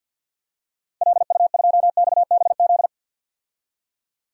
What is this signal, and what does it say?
Morse code, heard as a keyed tone, about 52 wpm, 700 Hz. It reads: LU2XBZ
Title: DXCC-100 Call Signs - 6 Characters - 50wpm